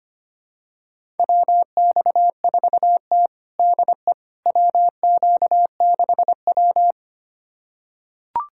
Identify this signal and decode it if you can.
Morse code, heard as a keyed tone, about 25 wpm, 700 Hz. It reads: WX4T DE WQ6W
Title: USA Callsign DE Callsign - 4 Characters - 25wpm